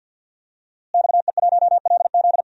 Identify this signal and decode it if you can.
Morse code, heard as a keyed tone, about 51 wpm, 700 Hz. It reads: XE1LZ
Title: DXCC-100 Call Signs - 5 Characters - Repeated 2x - 50wpm